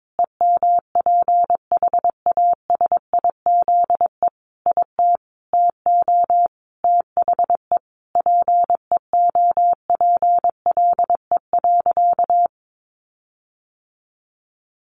Morse code, heard as a keyed tone, about 22 wpm, 700 Hz. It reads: EMPHASIZE IT TO THE PEOPLE.
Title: Sentences from Top 1500 Words 22wpm